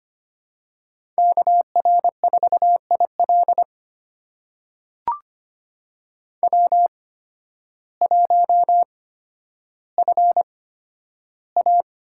Morse code, heard as a keyed tone, about 25 wpm, spaced as for 12 wpm, 700 Hz. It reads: KR4IL W1FA
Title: USA Call Signs - 8x Character Spacing - 25wpm